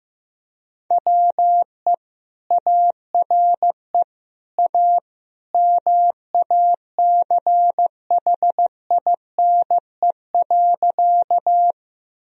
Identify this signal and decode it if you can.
Morse code, heard as a keyed tone, about 16 wpm, 700 Hz. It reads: WE ARE A MACHINE.
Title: Sentences from Top 1000 Words - Review Part 2 - 15wpm